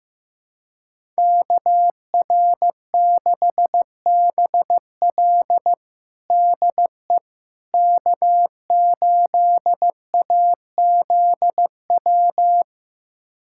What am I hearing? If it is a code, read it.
Morse code, heard as a keyed tone, about 15 wpm, 700 Hz. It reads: KR6BL DE K8AZW
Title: USA Callsign DE Callsign - 5 Characters - 15wpm